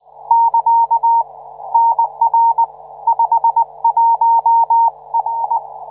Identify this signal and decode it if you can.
Morse code, heard as a keyed tone, about 19 wpm, 920 Hz. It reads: <KA> NR 51R